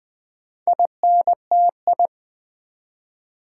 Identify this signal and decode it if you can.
Morse code, heard as a keyed tone, about 20 wpm, 700 Hz. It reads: INTI
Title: Build A Word - 2 Letters At A Time - 8-Letter Words - 20wpm